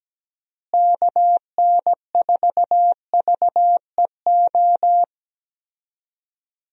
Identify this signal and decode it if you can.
Morse code, heard as a keyed tone, about 17 wpm, 700 Hz. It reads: KN4VEO
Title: USA Call Signs - 6 Characters - Repeated 3x - 17wpm